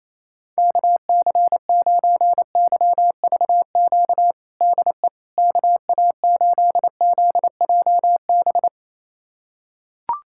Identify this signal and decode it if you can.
Morse code, heard as a keyed tone, about 28 wpm, 700 Hz. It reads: KC9YVQ DE KA8ZJB
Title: USA Callsign DE Callsign - 6 Characters - 28wpm